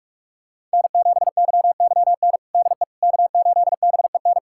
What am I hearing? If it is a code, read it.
Morse code, heard as a keyed tone, about 46 wpm, 700 Hz. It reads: N7YYN DE K8BEN
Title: USA Callsign DE Callsign - 5 Characters - 45wpm